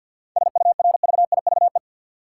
Morse code, heard as a keyed tone, about 47 wpm, 700 Hz. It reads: SURVIVE
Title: Sentences from Top 1300 Words - New Word Drill - 50wpm